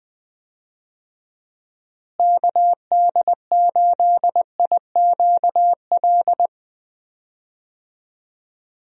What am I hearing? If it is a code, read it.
Morse code, heard as a keyed tone, about 21 wpm, 700 Hz. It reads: KD8IQL
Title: USA Call Signs - 6 Characters - Repeated 3x - 20wpm